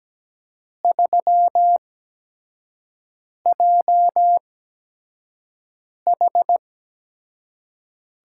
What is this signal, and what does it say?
Morse code, heard as a keyed tone, about 17 wpm, spaced as for 5 wpm, 700 Hz. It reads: 3JH
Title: USA Call Signs - 6 Characters - 8x Character Spacing - 17wpm